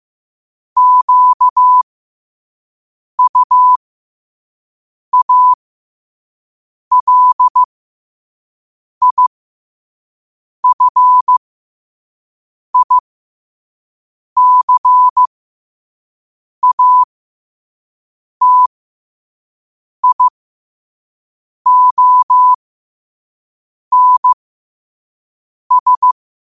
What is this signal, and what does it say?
Morse code, heard as a keyed tone, about 15 wpm, spaced as for 5 wpm, 1000 Hz. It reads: QUALIFICATIONS
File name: qotdfiles / 5wpm